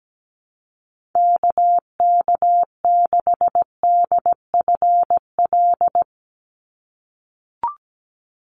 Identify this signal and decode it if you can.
Morse code, heard as a keyed tone, about 17 wpm, 700 Hz. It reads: KK6DFL
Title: USA Call Signs 17wpm